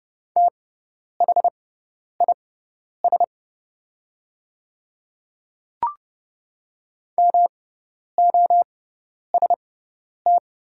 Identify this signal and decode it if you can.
Morse code, heard as a keyed tone, about 30 wpm, spaced as for 10 wpm, 690 Hz. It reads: THIS MOST
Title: 6x Character Spacing - Top 100 Words - 30wpm